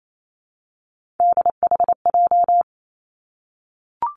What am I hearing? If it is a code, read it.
Morse code, heard as a keyed tone, about 28 wpm, 700 Hz. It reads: DHJ
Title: Three Challenging Letter-Number Combinations 28wpm